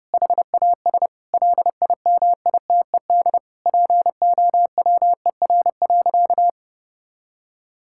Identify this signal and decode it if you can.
Morse code, heard as a keyed tone, about 30 wpm, 700 Hz. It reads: HAS LIMITED POWER.
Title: Sentences from Top 1000 Words - Review Part 6 - 30wpm